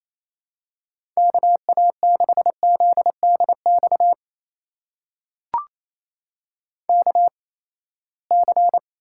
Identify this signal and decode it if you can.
Morse code, heard as a keyed tone, about 28 wpm, spaced as for 19 wpm, 700 Hz. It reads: KA6ZDX KC
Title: USA Call Signs - 6 Characters - 8x Character Spacing - 28wpm